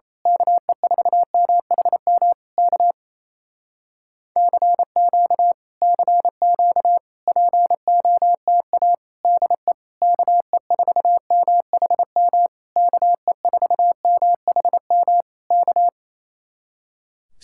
Morse code, heard as a keyed tone, about 26 wpm, 700 Hz. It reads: KE4MHM K CQ CQ POTA DE KE4MHM KE4MHM K
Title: Speed-Racing Common QSO Phrases Encore - 28wpm